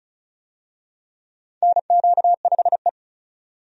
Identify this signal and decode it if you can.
Morse code, heard as a keyed tone, about 36 wpm, 700 Hz. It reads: NQ5E
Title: USA Call Signs - 4 Characters - 35wpm